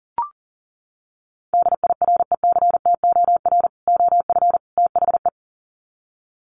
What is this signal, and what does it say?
Morse code, heard as a keyed tone, about 40 wpm, 700 Hz. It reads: DIRECTOR OF THE
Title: Sets of 3 Words 40wpm